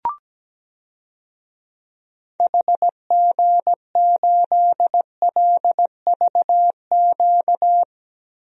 Morse code, heard as a keyed tone, about 17 wpm, 700 Hz. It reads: HG8LVQ
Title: DXCC-100 Call Signs - 6 Characters - Repeated 3x - 17wpm